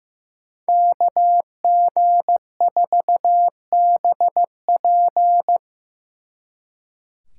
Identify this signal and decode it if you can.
Morse code, heard as a keyed tone, about 15 wpm, 700 Hz. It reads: KG4BP
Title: Speed Racing - USA 2x2 Call Signs 15wpm